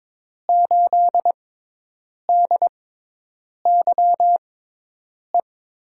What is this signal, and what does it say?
Morse code, heard as a keyed tone, about 21 wpm, spaced as for 8 wpm, 700 Hz. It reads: 8DYE
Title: USA Call Signs - 5 Characters - 6x Character Spacing - 22wpm